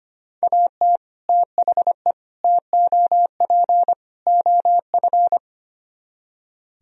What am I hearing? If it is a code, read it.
Morse code, heard as a keyed tone, about 32 wpm, 700 Hz. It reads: AT THE TOP OF